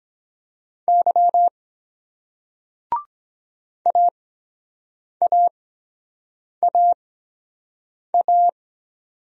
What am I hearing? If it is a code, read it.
Morse code, heard as a keyed tone, about 23 wpm, spaced as for 7 wpm, 700 Hz. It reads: Y AAAA